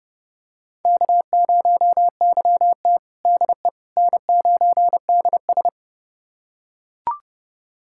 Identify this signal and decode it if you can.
Morse code, heard as a keyed tone, about 30 wpm, 700 Hz. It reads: K0YT DE N9DS
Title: USA Callsign DE Callsign - 4 Characters - 30wpm